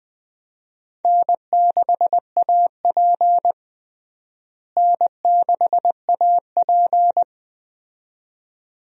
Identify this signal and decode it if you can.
Morse code, heard as a keyed tone, about 20 wpm, 700 Hz. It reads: N6AP N6AP
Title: USA Call Signs - 4 Characters - Repeated 2x - 20wpm